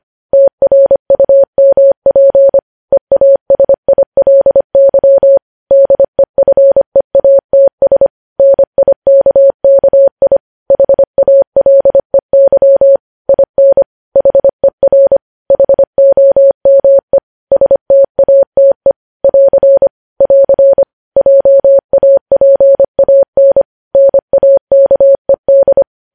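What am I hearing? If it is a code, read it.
Morse code, heard as a keyed tone, about 25 wpm, 550 Hz. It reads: TRUMP EASILY DEFEATS NIKKI HALEY IN HER HOME STATE <AR> <AR> JAPAN NAKED
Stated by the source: Morse code practice of news headlines at 25 wpm, opening with vvv.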